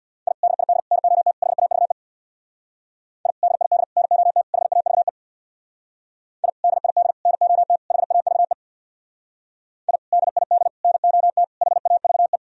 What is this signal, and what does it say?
Morse code, heard as a keyed tone, about 64 wpm, 700 Hz. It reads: I DID NOT HAVE I DID NOT HAVE I DID NOT HAVE I DID NOT HAVE